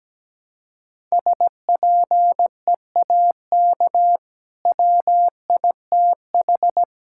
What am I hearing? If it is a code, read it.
Morse code, heard as a keyed tone, about 17 wpm, 700 Hz. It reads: SPEAK WITH
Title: Sentences from Top 1000 Words - Review Part 6 - 17wpm